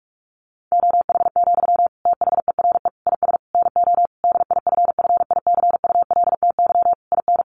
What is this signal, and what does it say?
Morse code, heard as a keyed tone, about 44 wpm, 700 Hz. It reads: OH, THERE IS NO DIFFICULTY IN
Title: Sentences from Top 1500 Words 45wpm